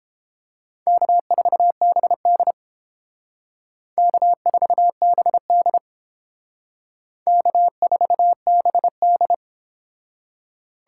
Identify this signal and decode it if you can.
Morse code, heard as a keyed tone, about 29 wpm, 700 Hz. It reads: K4BD K4BD K4BD